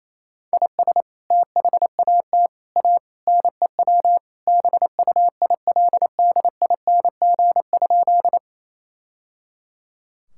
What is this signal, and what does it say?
Morse code, heard as a keyed tone, about 28 wpm, 700 Hz. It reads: IS THAT A NEW BUILDING?
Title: Sentences from Top 1000 Words - Review Part 5 - 28wpm